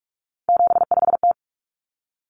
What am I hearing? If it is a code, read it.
Morse code, heard as a keyed tone, about 44 wpm, 700 Hz. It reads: 75T
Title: Common Cut Numbers - 3-Digit 45wpm